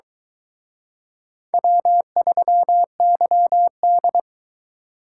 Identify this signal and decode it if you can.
Morse code, heard as a keyed tone, about 23 wpm, 700 Hz. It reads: W3YD